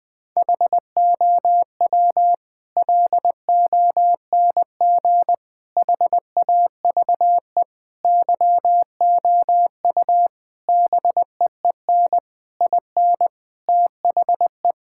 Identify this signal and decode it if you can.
Morse code, heard as a keyed tone, about 20 wpm, 700 Hz. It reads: HOW LONG HAVE YOU BEEN IN THE
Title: Sentences from Top 1000 Words - Review Part 6 - 20wpm